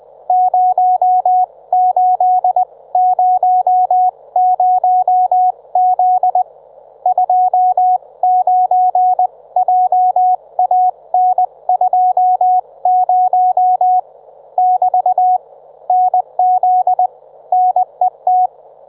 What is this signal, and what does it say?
Morse code, heard as a keyed tone, about 20 wpm, spaced as for 18 wpm, 730 Hz, with radio noise. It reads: 0800Z 29JAN20 <BT> NZ NET
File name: Trivia